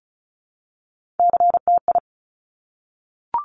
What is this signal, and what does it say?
Morse code, heard as a keyed tone, about 36 wpm, 700 Hz. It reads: CTI
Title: Top Three Letter Combinations in Words 35wpm